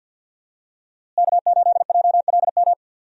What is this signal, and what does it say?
Morse code, heard as a keyed tone, about 51 wpm, 700 Hz. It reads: K8JLM
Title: USA Call Signs - 5 Characters - Repeated 2x - 50wpm